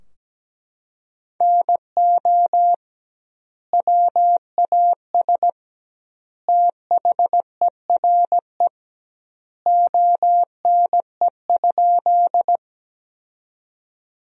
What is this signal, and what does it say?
Morse code, heard as a keyed tone, about 17 wpm, 700 Hz. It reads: NO WAS THERE ONE?
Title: 2x Word-Spacing - Sentences from Top 100 Words - 17wpm